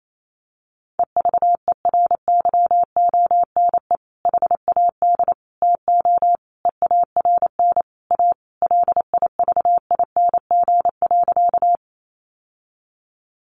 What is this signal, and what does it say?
Morse code, heard as a keyed tone, about 28 wpm, 700 Hz. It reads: EVERYONE HAD TO EARN A LIVING.
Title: Sentences from Top 1500 Words 28wpm